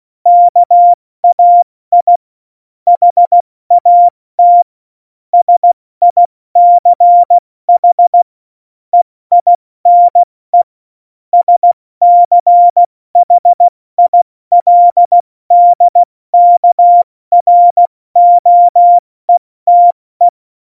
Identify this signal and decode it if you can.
Morse code, heard as a keyed tone, about 16 wpm, 700 Hz. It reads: KAI HAT SICH EINE SCHILDKROETE
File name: Morsekurs / Witze / Witz 14. November 2017, 70 F80